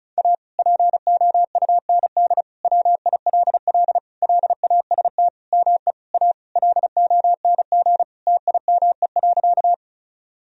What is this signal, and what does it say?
Morse code, heard as a keyed tone, about 36 wpm, 700 Hz. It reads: A POUND WILL LAST ME A LONG TIME.
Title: Sentences from Top 1000 Words - Review Part 2 - 35wpm